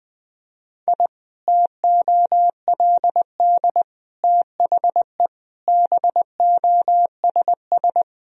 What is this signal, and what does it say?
Morse code, heard as a keyed tone, about 20 wpm, 700 Hz. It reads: I TOLD THE BOSS
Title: Sentences from Top 1000 Words - Review Part 4 - 20wpm